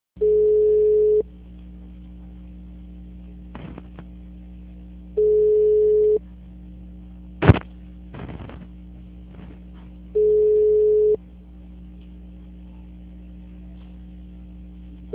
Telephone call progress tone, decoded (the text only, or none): ringback tone